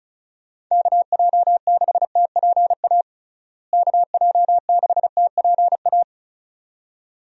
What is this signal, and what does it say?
Morse code, heard as a keyed tone, about 36 wpm, 700 Hz. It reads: KJ6TPA KJ6TPA
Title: USA Call Signs - 6 Characters - Repeated 2x - 35wpm